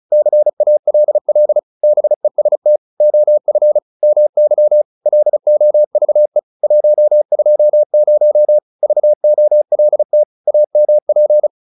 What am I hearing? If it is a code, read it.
Morse code, heard as a keyed tone, about 34 wpm, 600 Hz. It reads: CALL BEST OF MY LOVE 120 VOLT AMP